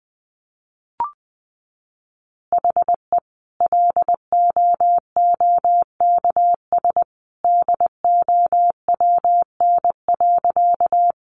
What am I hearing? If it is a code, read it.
Morse code, heard as a keyed tone, about 20 wpm, 700 Hz. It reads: HE LOOKS DOWN.
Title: Additional Easy Phrases 20wpm